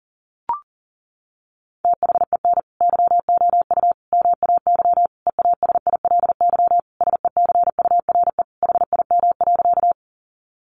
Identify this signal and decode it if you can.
Morse code, heard as a keyed tone, about 40 wpm, 700 Hz. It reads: THEN YOU MAY EASILY SECURE HIM.
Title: Sentences from Top 1200 Words 40wpm